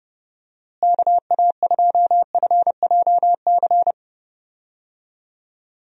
Morse code, heard as a keyed tone, about 30 wpm, 700 Hz. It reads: KA2FJC
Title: USA Call Signs - 6 Characters - Repeated 2x - 30wpm